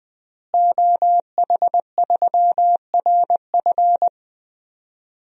Morse code, heard as a keyed tone, about 20 wpm, 700 Hz. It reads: OH3RF
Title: DXCC-100 Call Signs - 5 Characters - Repeated 2x - 20wpm